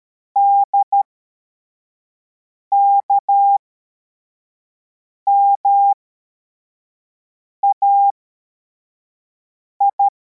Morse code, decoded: DKMAI